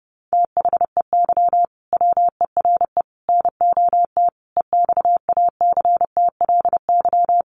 Morse code, heard as a keyed tone, about 30 wpm, 700 Hz. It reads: THEY WERE NOT EXACTLY
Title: Sentences from Top 1000 Words 30wpm